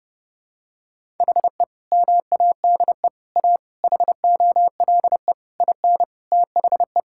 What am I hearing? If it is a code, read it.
Morse code, heard as a keyed tone, about 30 wpm, 700 Hz. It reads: HE MADE A HOLE IN THE
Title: Sentences from Top 1600 Words 30wpm